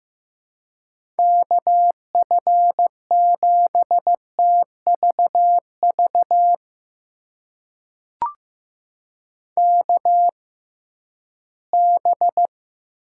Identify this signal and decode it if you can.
Morse code, heard as a keyed tone, about 15 wpm, spaced as for 11 wpm, 700 Hz. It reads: KF7TVV KB
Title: USA Call Signs - 6 Characters - 6x Character Spacing - 15wpm